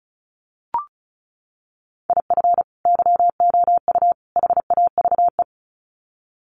Morse code, heard as a keyed tone, about 34 wpm, 700 Hz. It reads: IF YOU HAVE
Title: Sets of 3 Words 35wpm